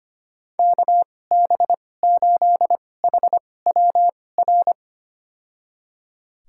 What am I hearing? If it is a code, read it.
Morse code, heard as a keyed tone, about 25 wpm, spaced as for 18 wpm, 700 Hz. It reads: KB8HWR
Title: USA Call Signs - 6 Characters - 2x Character Spacing - 25wpm